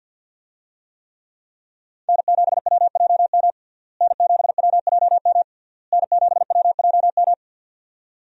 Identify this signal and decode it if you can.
Morse code, heard as a keyed tone, about 50 wpm, 700 Hz. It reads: N7WJM N7WJM N7WJM
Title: USA Call Signs - 5 Characters - Repeated 3x - 50wpm